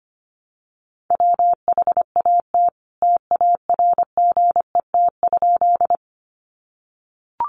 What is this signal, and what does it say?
Morse code, heard as a keyed tone, about 25 wpm, 700 Hz. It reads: WHAT TARGET?
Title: Sentences from Top 1200 Words 25wpm